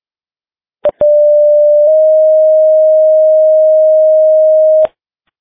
Two-tone page, heard as a one-second tone, then a three-second tone, about 590 then 620 Hz. It is heard from 1.0 to 4.9 s.